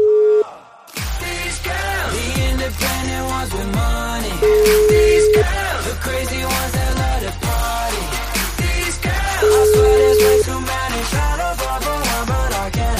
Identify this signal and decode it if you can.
Ringback tone (Europe, 425 Hz), with noise.